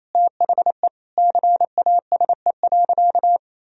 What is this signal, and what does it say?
Morse code, heard as a keyed tone, about 28 wpm, 700 Hz. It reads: THE CASE.
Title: Sentences from Top 1000 Words - Review Part 4 - 28wpm